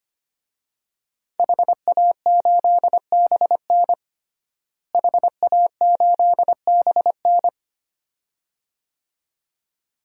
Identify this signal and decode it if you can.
Morse code, heard as a keyed tone, about 25 wpm, 700 Hz. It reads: HA8BN HA8BN
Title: DXCC-100 Call Signs - 5 Characters - Repeated 2x - 25wpm